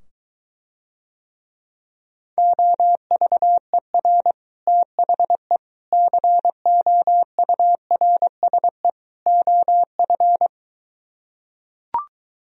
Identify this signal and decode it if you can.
Morse code, heard as a keyed tone, about 23 wpm, 700 Hz. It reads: OVER THE COURSE OF